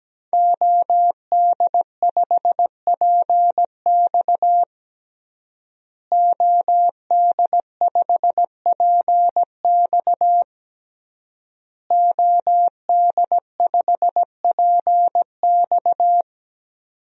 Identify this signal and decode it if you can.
Morse code, heard as a keyed tone, about 17 wpm, 700 Hz. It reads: OD5PX OD5PX OD5PX
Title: DXCC-100 Call Signs - 5 Characters - Repeated 3x - 17wpm